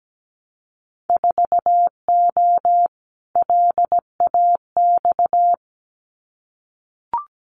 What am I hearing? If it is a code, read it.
Morse code, heard as a keyed tone, about 17 wpm, 700 Hz. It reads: 4O LAX
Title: Winter Field Day Exchanges Encore 17wpm